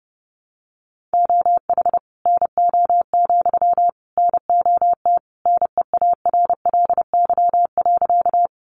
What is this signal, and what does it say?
Morse code, heard as a keyed tone, about 30 wpm, 700 Hz. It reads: OH NO, NOT NEARLY.
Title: Sentences from Top 1300 Words 30wpm